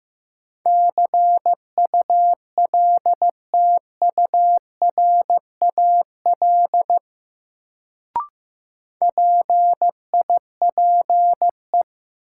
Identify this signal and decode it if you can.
Morse code, heard as a keyed tone, about 15 wpm, 700 Hz. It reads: CULTURAL PIPE
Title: Sentences from Top 1500 Words - New Word Drill Alt - 15wpm